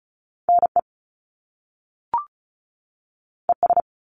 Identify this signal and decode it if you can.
Morse code, heard as a keyed tone, about 36 wpm, 700 Hz. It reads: NE ES
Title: Top Two Letter Combinations in Words 35wpm